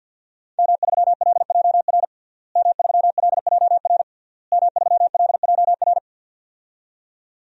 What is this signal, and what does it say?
Morse code, heard as a keyed tone, about 51 wpm, 700 Hz. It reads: M3LJR M3LJR M3LJR
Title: DXCC-100 Call Signs - 5 Characters - Repeated 3x - 50wpm